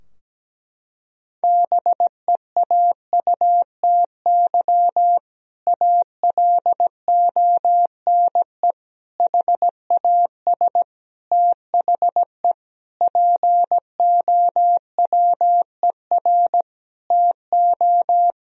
Morse code, heard as a keyed tone, about 17 wpm, 700 Hz. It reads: BEAUTY ALONE HAS THE POWER TO
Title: Sentences from Top 1000 Words - Review Part 2 - 17wpm